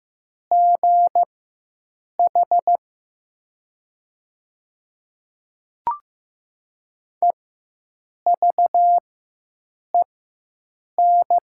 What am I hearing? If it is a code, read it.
Morse code, heard as a keyed tone, about 15 wpm, spaced as for 7 wpm, 700 Hz. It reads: GH EVEN
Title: 4x Character Spacing - Top 300 Words - 15wpm